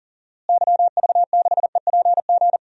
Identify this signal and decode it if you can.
Morse code, heard as a keyed tone, about 37 wpm, 700 Hz. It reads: YV6EPG
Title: DXCC-100 Call Signs - 6 Characters - Repeated 2x - 40wpm